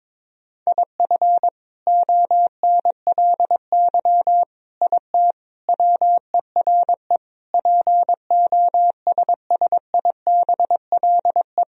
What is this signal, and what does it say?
Morse code, heard as a keyed tone, about 22 wpm, 700 Hz. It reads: IF ONLY IT WERE POSSIBLE
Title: Sentences from Top 500 Words and Abbreviations 22wpm